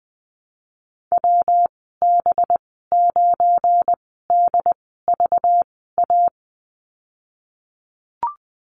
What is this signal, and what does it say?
Morse code, heard as a keyed tone, about 20 wpm, 700 Hz. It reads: W B 9 D V A
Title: 207 Lesson 33g - Letter j - Call Signs 20wpm